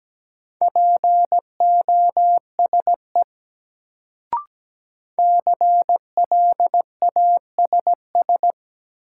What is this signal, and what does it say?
Morse code, heard as a keyed tone, about 17 wpm, 700 Hz. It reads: POSE CLASS
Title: Sentences from Top 1000 Words - Review Part 2 - New Word Drill - 17wpm